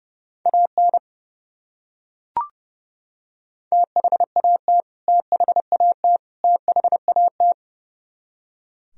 Morse code, heard as a keyed tone, about 32 wpm, 700 Hz. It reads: AN THAT THAT THAT
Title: Top 100 Words Repeated 30wpm